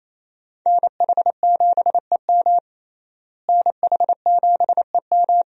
Morse code, heard as a keyed tone, about 28 wpm, 700 Hz. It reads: NH7EM NH7EM
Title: USA Call Signs - 5 Characters - Repeated 3x - 28wpm